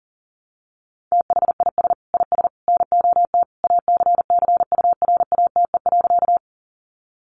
Morse code, heard as a keyed tone, about 40 wpm, 700 Hz. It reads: THIS IS NOT ACCURATE.
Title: Sentences from Top 1200 Words 40wpm